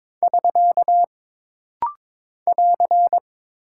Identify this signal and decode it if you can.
Morse code, heard as a keyed tone, about 22 wpm, 700 Hz. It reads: <SK> <AR>
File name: practice / 22wpm / Single Punctuation-Prosign - Rapid-Fire - 22wpm